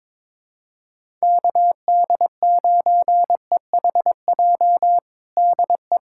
Morse code, heard as a keyed tone, about 22 wpm, 700 Hz. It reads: KD9EHJ DE
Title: USA Callsign DE Callsign 22wpm